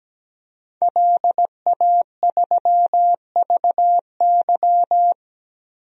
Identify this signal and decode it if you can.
Morse code, heard as a keyed tone, about 17 wpm, 700 Hz. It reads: LA3VY
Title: DXCC-100 Call Signs - 5 Characters - 17wpm